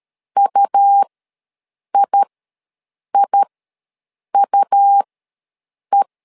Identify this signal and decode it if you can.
Morse code, heard as a keyed tone, about 13 wpm, spaced as for 7 wpm, 800 Hz. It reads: UIIUE
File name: LEZIONI / 2 UHDN / 2 U ws 12wpm 1655